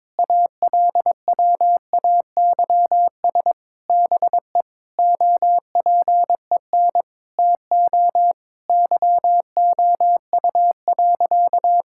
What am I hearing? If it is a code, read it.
Morse code, heard as a keyed tone, about 22 wpm, 700 Hz. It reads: ALWAYS BE OPEN TO YOU.
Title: Sentences from Top 1600 Words 22wpm